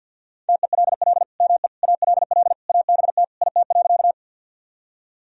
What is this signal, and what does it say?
Morse code, heard as a keyed tone, about 46 wpm, 700 Hz. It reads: TELL ME ALL ABT IT.
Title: Sentences from Top 200 Words and Abbreviations 50wpm